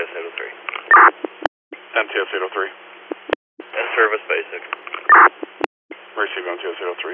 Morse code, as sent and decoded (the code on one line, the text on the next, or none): none